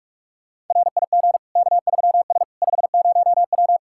Morse code, heard as a keyed tone, about 46 wpm, 700 Hz. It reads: RIG K3S 50W
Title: Practice QSO - Exchange 2 - Info - 45wpm